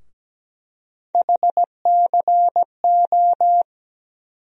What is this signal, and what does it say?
Morse code, heard as a keyed tone, about 17 wpm, 700 Hz. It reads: HCO
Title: USA Call Sign Postfixes - 3 Letters 17wpm